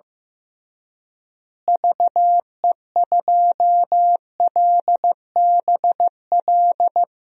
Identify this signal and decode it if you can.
Morse code, heard as a keyed tone, about 15 wpm, 700 Hz. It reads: VE2LBL
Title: DXCC-100 Call Signs - 6 Characters - Repeated 3x - 15wpm